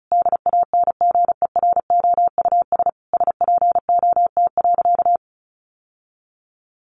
Morse code, heard as a keyed tone, about 34 wpm, 700 Hz. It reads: DANGEROUS SPOT.
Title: Sentences from Top 1200 Words 35wpm